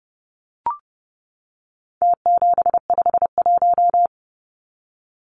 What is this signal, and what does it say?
Morse code, heard as a keyed tone, about 30 wpm, 700 Hz. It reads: T751
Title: Numbers - Contest Style - 4-Digit 30wpm